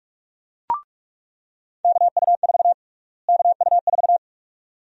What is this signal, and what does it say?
Morse code, heard as a keyed tone, about 46 wpm, 700 Hz. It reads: KU4 KU4
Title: USA Call Sign Prefixes - Repeated 2x - 45wpm